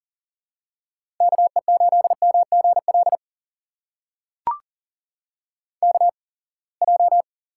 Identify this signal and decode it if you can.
Morse code, heard as a keyed tone, about 38 wpm, spaced as for 27 wpm, 700 Hz. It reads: KE8MGL KJ
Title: USA Call Signs - 6 Characters - 8x Character Spacing - 40wpm